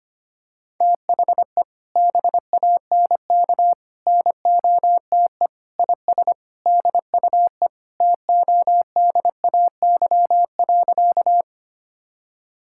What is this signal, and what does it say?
Morse code, heard as a keyed tone, about 25 wpm, 700 Hz. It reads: THE BANK NOTE IS DUE TODAY.
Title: Sentences from Top 1000 Words - Review Part 6 - 25wpm